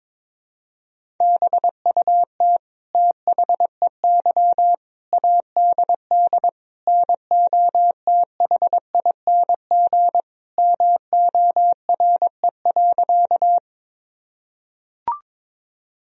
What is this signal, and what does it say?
Morse code, heard as a keyed tone, about 22 wpm, 700 Hz. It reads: BUT THEY ADD NOTHING MORE.
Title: Sentences from Top 1000 Words - Review Part 1 - 22wpm